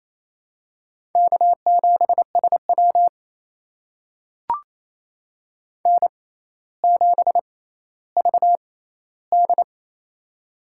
Morse code, heard as a keyed tone, about 28 wpm, spaced as for 14 wpm, 700 Hz. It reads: K7SW N7VD
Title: USA Call Signs - 4 Characters - 6x Character Spacing - 28wpm